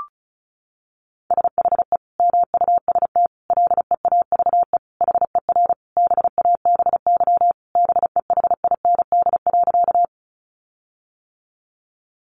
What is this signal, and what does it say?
Morse code, heard as a keyed tone, about 34 wpm, 700 Hz. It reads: SHE MUST LEAVE HER BABY BEHIND.
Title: Sentences from Top 1300 Words 35wpm